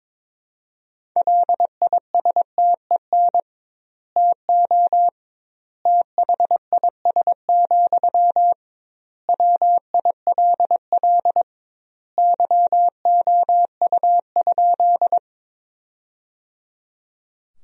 Morse code, heard as a keyed tone, about 22 wpm, 700 Hz. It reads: LISTEN TO THIS, WILL YOU?
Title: 2x Word-Spacing - Sentences from Top 400 Words - 22wpm